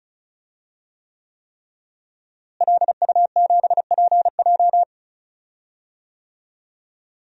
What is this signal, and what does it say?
Morse code, heard as a keyed tone, about 34 wpm, 700 Hz. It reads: LU7PJ
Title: DXCC-100 Call Signs - 5 Characters - Repeated 3x - 35wpm